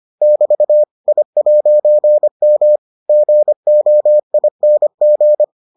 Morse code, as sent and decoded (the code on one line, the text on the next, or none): -...- / .. .----. -- / --. --- .. -. --.
<BT> I'M GOING